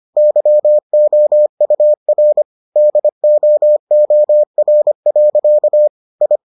Morse code, heard as a keyed tone, about 25 wpm, 600 Hz. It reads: YOUR DOOR. I